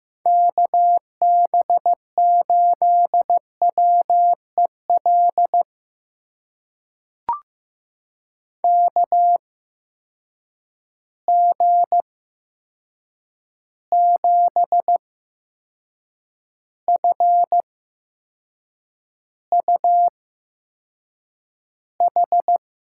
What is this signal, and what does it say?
Morse code, heard as a keyed tone, about 15 wpm, spaced as for 6 wpm, 700 Hz. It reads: KB8WEL KG7FUH
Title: USA Call Signs - 6 Characters - 8x Character Spacing - 15wpm